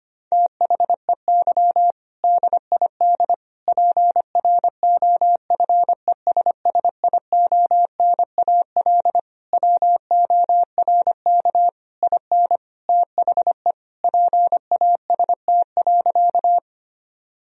Morse code, heard as a keyed tone, about 25 wpm, 700 Hz. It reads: THEY DID PROFESSIONAL WORK IN THE PAST.
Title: Sentences from Top 1000 Words - Review Part 4 - 25wpm